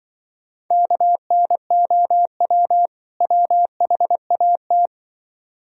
Morse code, heard as a keyed tone, about 23 wpm, 700 Hz. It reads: KNOW WHAT
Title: Speed-Racing - Sets of 2 Words 20wpm